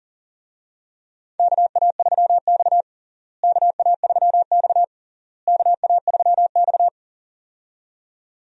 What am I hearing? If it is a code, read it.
Morse code, heard as a keyed tone, about 40 wpm, 700 Hz. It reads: KA3X KA3X KA3X
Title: USA Call Signs - 4 Characters - Repeated 3x - 40wpm